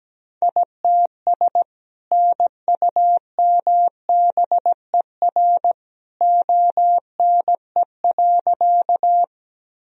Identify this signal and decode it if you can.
Morse code, heard as a keyed tone, about 17 wpm, 700 Hz. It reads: ITS NUMBER ONE.
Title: Sentences from Top 1000 Words - Review Part 1 - 17wpm